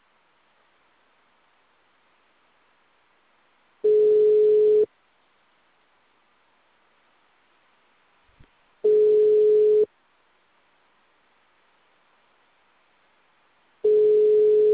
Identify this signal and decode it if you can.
Ringback tone (Europe, 425 Hz).